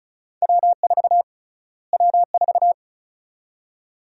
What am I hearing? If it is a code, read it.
Morse code, heard as a keyed tone, about 36 wpm, 700 Hz. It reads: W4 W4
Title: USA Call Sign Prefixes - Repeated 2x - 35wpm